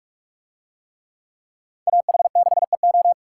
Morse code, heard as a keyed tone, about 39 wpm, 700 Hz. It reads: AH6EO